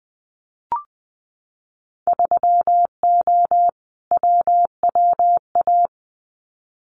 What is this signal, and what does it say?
Morse code, heard as a keyed tone, about 20 wpm, 700 Hz. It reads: 3O WWA
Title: Winter Field Day Exchanges Encore 20wpm